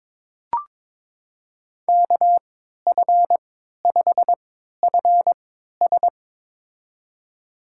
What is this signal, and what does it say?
Morse code, heard as a keyed tone, about 21 wpm, spaced as for 12 wpm, 700 Hz. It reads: KF5FS
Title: USA Call Signs - 3x Character Spacing - 22wpm